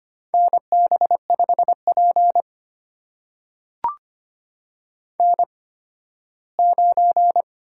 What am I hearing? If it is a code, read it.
Morse code, heard as a keyed tone, about 25 wpm, spaced as for 15 wpm, 700 Hz. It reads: NB5P N9
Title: USA Call Signs - 8x Character Spacing - 25wpm